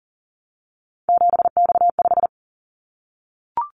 Morse code, heard as a keyed tone, about 38 wpm, 700 Hz. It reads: 7X5
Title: Callsign Prefixes for Top 100 DXCC - 40wpm